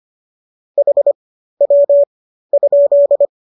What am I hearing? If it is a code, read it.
Morse code, heard as a keyed tone, about 25 wpm, spaced as for 13 wpm, 570 Hz. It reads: HW?